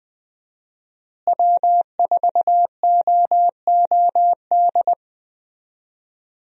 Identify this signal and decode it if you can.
Morse code, heard as a keyed tone, about 20 wpm, 700 Hz. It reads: W4OOD
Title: USA Call Signs - 5 Characters - Repeated 2x - 20wpm